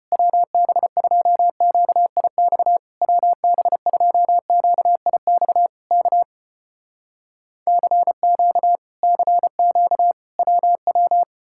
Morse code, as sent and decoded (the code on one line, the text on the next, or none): .-- -... ..--- --.- .. -..- / .-- -... ..--- --.- .. -..- / -.- / -.-. --.- / -.-. --.- / .-- .--
WB2QIX WB2QIX K CQ CQ WW